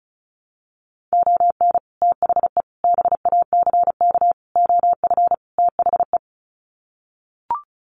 Morse code, heard as a keyed tone, about 34 wpm, 700 Hz. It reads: ON THE BACK OF THE
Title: Sets of 5 Words 35wpm